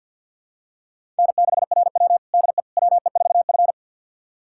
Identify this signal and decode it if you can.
Morse code, heard as a keyed tone, about 50 wpm, 700 Hz. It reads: N6RW DE WE4F
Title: USA Callsign DE Callsign - 4 Characters - 50wpm